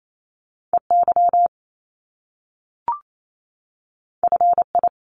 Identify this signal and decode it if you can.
Morse code, heard as a keyed tone, about 28 wpm, 700 Hz. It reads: EY FI
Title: Top Two Letter Combinations in Words 28wpm